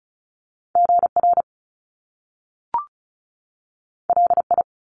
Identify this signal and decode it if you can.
Morse code, heard as a keyed tone, about 36 wpm, 700 Hz. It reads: GR LI
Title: Top Two Letter Combinations in Words 35wpm